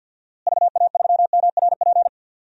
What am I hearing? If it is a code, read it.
Morse code, heard as a keyed tone, about 53 wpm, 700 Hz. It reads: VA3MRP
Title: DXCC-100 Call Signs - 6 Characters - Repeated 2x - 50wpm